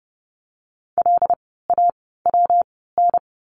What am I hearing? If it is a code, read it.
Morse code, heard as a keyed tone, about 30 wpm, spaced as for 17 wpm, 700 Hz. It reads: LAWN
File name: course / 30:10wpm / 115 Lesson 19f - Letter w - Words 30wpm